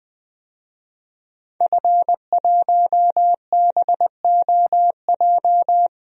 Morse code, heard as a keyed tone, about 21 wpm, 700 Hz. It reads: F1BOJ
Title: DXCC-100 Call Signs - 5 Characters - Repeated 3x - 20wpm